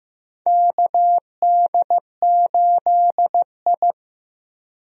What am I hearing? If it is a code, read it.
Morse code, heard as a keyed tone, about 15 wpm, 700 Hz. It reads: KD8I
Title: USA Call Signs - 4 Characters - Repeated 2x - 15wpm